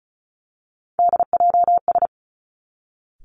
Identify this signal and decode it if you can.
Morse code, heard as a keyed tone, about 36 wpm, 700 Hz. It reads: DJS
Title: Three Challenging Letter-Number Combinations 35wpm